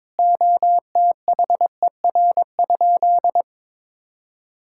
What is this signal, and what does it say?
Morse code, heard as a keyed tone, about 21 wpm, 700 Hz. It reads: OTHER?
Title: Sentences from Top 1000 Words - Review Part 6 - 22wpm